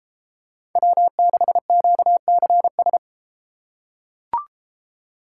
Morse code, heard as a keyed tone, about 33 wpm, 700 Hz. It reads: W6QCS